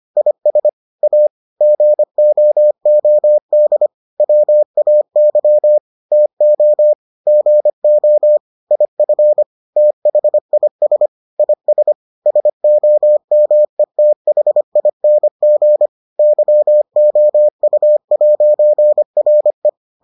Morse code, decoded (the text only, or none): IS A GOOD WAY TO GO IF THIS IS SOMETHING YOU'RE